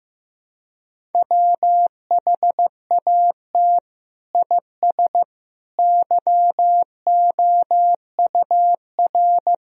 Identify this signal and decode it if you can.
Morse code, heard as a keyed tone, about 16 wpm, 700 Hz. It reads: WHAT IS YOUR
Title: Sentences from Top 1000 Words - Review Part 1 - 15wpm